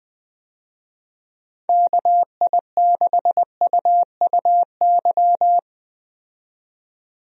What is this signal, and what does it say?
Morse code, heard as a keyed tone, about 20 wpm, 700 Hz. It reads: KI6UUY
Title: USA Call Signs - 6 Characters - 20wpm